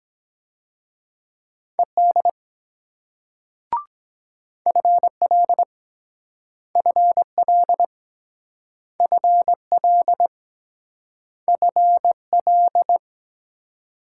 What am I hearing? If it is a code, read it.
Morse code, heard as a keyed tone, about 23 wpm, 700 Hz. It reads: ED FL FL FL FL